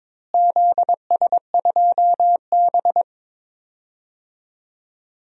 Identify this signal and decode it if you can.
Morse code, heard as a keyed tone, about 22 wpm, 700 Hz. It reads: ZS2B
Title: DXCC-100 Call Signs - 4 Characters - Repeated 3x - 22wpm